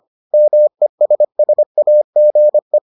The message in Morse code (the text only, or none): MESSAGE